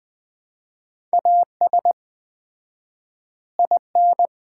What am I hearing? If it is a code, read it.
Morse code, heard as a keyed tone, about 20 wpm, 700 Hz. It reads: AS IN
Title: 4x Word-Spacing - Sets of 5 Words - 20wpm